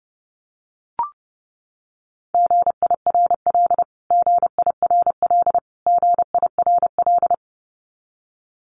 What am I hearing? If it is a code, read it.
Morse code, heard as a keyed tone, about 30 wpm, 700 Hz. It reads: GIRL GIRL GIRL
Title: Top 500 Words Repeated 30wpm